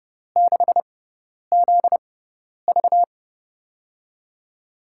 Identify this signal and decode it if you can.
Morse code, heard as a keyed tone, about 31 wpm, spaced as for 10 wpm, 700 Hz. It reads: 6ZV
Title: USA Call Signs - 4 Characters - 6x Character Spacing - 30wpm